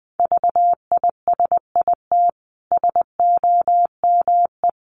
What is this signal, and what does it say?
Morse code, heard as a keyed tone, about 20 wpm, 700 Hz. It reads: VISIT SOME
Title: Sentences from Top 900 Words 20wpm